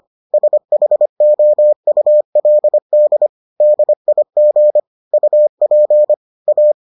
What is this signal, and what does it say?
Morse code, heard as a keyed tone, about 25 wpm, 600 Hz. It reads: SHOULD DIG UP A